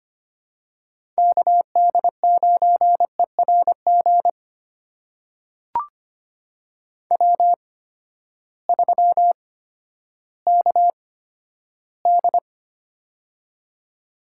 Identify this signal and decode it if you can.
Morse code, heard as a keyed tone, about 25 wpm, spaced as for 12 wpm, 700 Hz. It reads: KD9ERG W3KD
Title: USA Call Signs - 8x Character Spacing - 25wpm